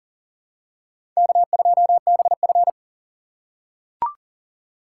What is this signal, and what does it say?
Morse code, heard as a keyed tone, about 40 wpm, 700 Hz. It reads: K2BF
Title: USA Call Signs - 4 Characters - 6x Character Spacing - 40wpm